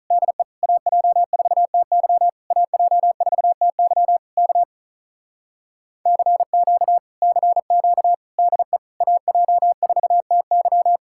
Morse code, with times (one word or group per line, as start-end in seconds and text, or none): 0.1-0.4 DE
0.6-2.3 AJ4TY
2.5-4.2 AJ4TY
4.4-4.6 K
6.1-7.0 CQ
7.2-8.1 CQ
8.4-8.8 DE
9.0-11.0 AJ4TY